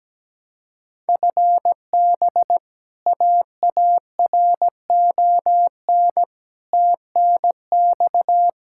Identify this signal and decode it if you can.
Morse code, heard as a keyed tone, about 17 wpm, 700 Hz. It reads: FB AARON TNX
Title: Practice QSO - Exchange 3 - More Info - 17wpm